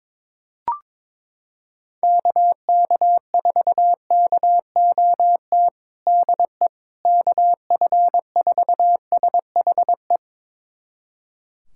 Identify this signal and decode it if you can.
Morse code, heard as a keyed tone, about 21 wpm, 700 Hz. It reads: KK4KOT DE KF4SHE
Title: USA Callsign DE Callsign 22wpm